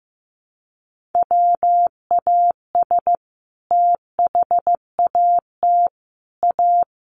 Morse code, heard as a keyed tone, about 15 wpm, 700 Hz. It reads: WAS THAT A
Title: Sentences from Top 1400 Words 15wpm